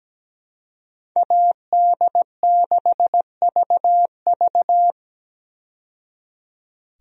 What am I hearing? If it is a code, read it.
Morse code, heard as a keyed tone, about 17 wpm, 700 Hz. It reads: AD6VV
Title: USA Call Signs - 5 Characters - 17wpm